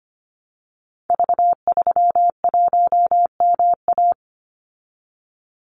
Morse code, heard as a keyed tone, about 25 wpm, 700 Hz. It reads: V31MA
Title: Call Signs - 25wpm